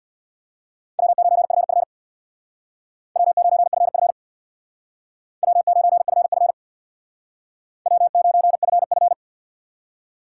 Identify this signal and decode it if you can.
Morse code, heard as a keyed tone, about 62 wpm, 700 Hz. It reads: W9FF W9FF W9FF W9FF
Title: Speed Racing - USA Call Signs 50wpm